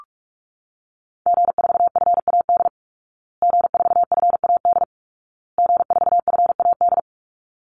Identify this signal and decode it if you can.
Morse code, heard as a keyed tone, about 44 wpm, 700 Hz. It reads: G4FAD G4FAD G4FAD
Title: Call Signs Repeated 2x - 45wpm